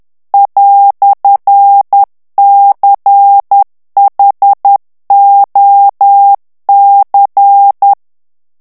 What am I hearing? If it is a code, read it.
Morse code, heard as a keyed tone, about 10 wpm, 800 Hz. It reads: "CHOC
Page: Français - texte en clair 1 - vitesse : 10 mots minute